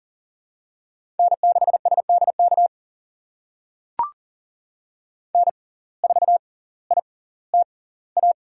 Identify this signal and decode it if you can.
Morse code, heard as a keyed tone, about 40 wpm, spaced as for 20 wpm, 700 Hz. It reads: N6SDK N4ITA
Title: USA Call Signs - 5 Characters - 6x Character Spacing - 40wpm